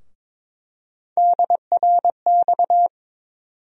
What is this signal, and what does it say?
Morse code, heard as a keyed tone, about 22 wpm, 700 Hz. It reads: DRX
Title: USA Call Sign Postfixes - 3 Letters 22wpm